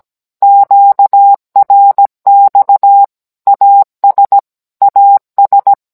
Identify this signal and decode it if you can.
Morse code, heard as a keyed tone, about 17 wpm, 800 Hz. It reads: QRX AS AS